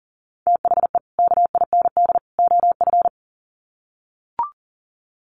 Morse code, heard as a keyed tone, about 40 wpm, 700 Hz. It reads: THE KIND OF
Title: Sets of 3 Words 40wpm